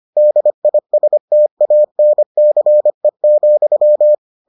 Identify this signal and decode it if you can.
Morse code, heard as a keyed tone, about 25 wpm, 600 Hz. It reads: DISTANCE,